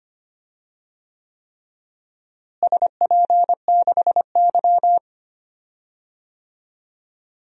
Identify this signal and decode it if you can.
Morse code, heard as a keyed tone, about 25 wpm, 700 Hz. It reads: SP6Y
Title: DXCC-100 Call Signs - 4 Characters - Repeated 3x - 25wpm